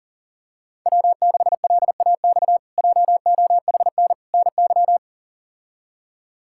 Morse code, heard as a keyed tone, about 40 wpm, 700 Hz. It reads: W6LAX JOHN NY
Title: K1USN SST QSO Practice - 40wpm